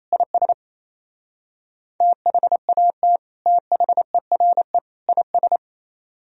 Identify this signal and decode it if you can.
Morse code, heard as a keyed tone, about 35 wpm, 700 Hz. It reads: IS THAT THERE IS
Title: Speed-Racing - Sets of 3 Words 28wpm